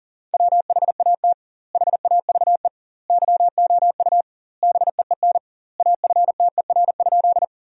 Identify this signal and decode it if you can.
Morse code, heard as a keyed tone, about 40 wpm, 700 Hz. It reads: WHAT HAVE YOU BEEN AFTER?
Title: 2x Word-Spacing - Sentences from Top 200 Words - 40wpm